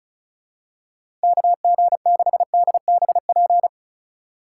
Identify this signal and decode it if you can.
Morse code, heard as a keyed tone, about 36 wpm, 700 Hz. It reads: KG6DBP
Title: USA Call Signs - 6 Characters - 4x Character Spacing - 35wpm